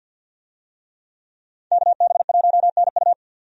Morse code, heard as a keyed tone, about 48 wpm, 700 Hz. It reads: KB1NU
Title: USA Call Signs - 5 Characters - 50wpm